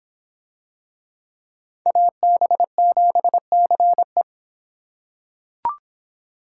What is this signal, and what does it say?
Morse code, heard as a keyed tone, about 25 wpm, 700 Hz. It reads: AB7CE